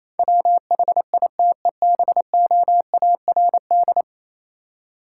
Morse code, heard as a keyed tone, about 28 wpm, 700 Hz. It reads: WHITEBOARD